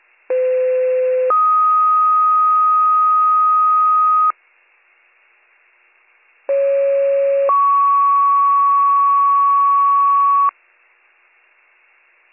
Two-tone page, heard as a one-second tone, then a three-second tone, about 510 then 1250 Hz, about 550 then 1120 Hz. It is heard from 0.3 to 10.5 s.